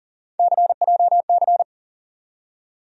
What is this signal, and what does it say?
Morse code, heard as a keyed tone, about 40 wpm, 700 Hz. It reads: CJC
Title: USA Call Sign Postfixes - 3 Letters 40wpm